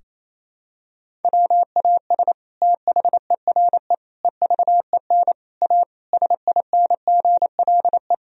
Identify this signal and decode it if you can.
Morse code, heard as a keyed tone, about 27 wpm, 700 Hz. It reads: WAS THERE EVEN A SINGLE
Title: Sentences from Top 1000 Words - Review Part 4 - 28wpm